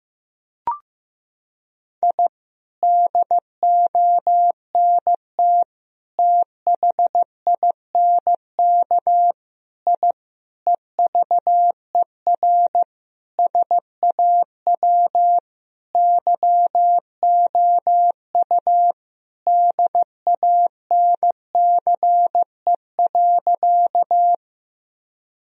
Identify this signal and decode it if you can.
Morse code, heard as a keyed tone, about 16 wpm, 700 Hz. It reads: I DONT THINK I EVER SAW YOU DANCE.
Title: Sentences from Top 1000 Words - Review Part 4 - 15wpm